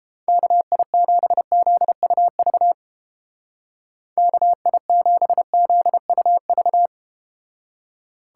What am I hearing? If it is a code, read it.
Morse code, heard as a keyed tone, about 32 wpm, 700 Hz. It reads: KI7ZUV KI7ZUV